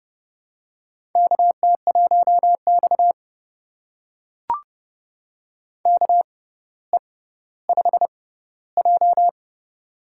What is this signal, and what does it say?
Morse code, heard as a keyed tone, about 31 wpm, spaced as for 16 wpm, 700 Hz. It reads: KT1X KE5J
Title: USA Call Signs - 4 Characters - 6x Character Spacing - 30wpm